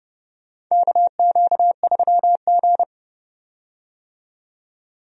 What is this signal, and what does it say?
Morse code, heard as a keyed tone, about 31 wpm, 700 Hz. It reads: KQ3G
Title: USA Call Signs - 4 Characters - 30wpm